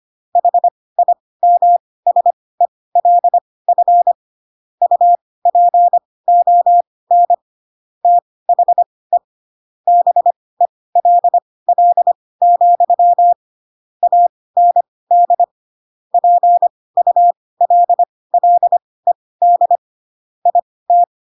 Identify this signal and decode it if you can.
Morse code, heard as a keyed tone, about 25 wpm, spaced as for 18 wpm, 700 Hz. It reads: HIMSELF UPON THE BELL, AND PULLED IT
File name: Pinocchio / Pinocchio ch6 12 25 ik1hge